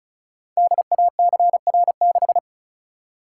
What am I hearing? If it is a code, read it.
Morse code, heard as a keyed tone, about 36 wpm, 690 Hz. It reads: DACR6
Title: Code Groups - 5 Letter:Numbers - 35wpm